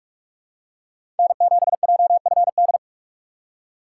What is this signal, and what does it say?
Morse code, heard as a keyed tone, about 47 wpm, 700 Hz. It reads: N7JFD
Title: USA Call Signs - 5 Characters - 6x Character Spacing - 45wpm